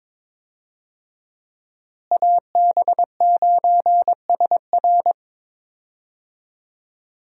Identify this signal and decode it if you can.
Morse code, heard as a keyed tone, about 22 wpm, 700 Hz. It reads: AB9SR
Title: USA Call Signs - 5 Characters - Repeated 2x - 22wpm